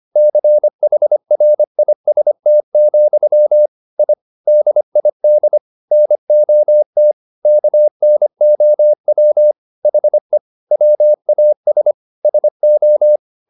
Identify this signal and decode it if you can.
Morse code, heard as a keyed tone, about 25 wpm, 600 Hz. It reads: CHRIST, I DID NOT KNOW HE WAS SO
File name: the old man and the sea part 1 25wpm